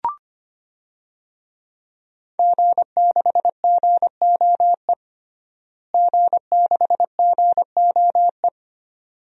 Morse code, heard as a keyed tone, about 25 wpm, 700 Hz. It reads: G6GOE G6GOE
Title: DXCC-100 Call Signs - 5 Characters - Repeated 2x - 25wpm